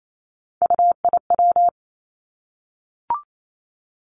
Morse code, heard as a keyed tone, about 28 wpm, 700 Hz. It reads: UIW
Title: Three Challenging Letter-Number Combinations 28wpm